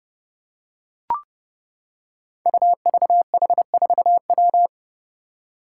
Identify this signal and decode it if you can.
Morse code, heard as a keyed tone, about 30 wpm, 700 Hz. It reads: UVH4W
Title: Five Challenging Letter-Number Combinations 30wpm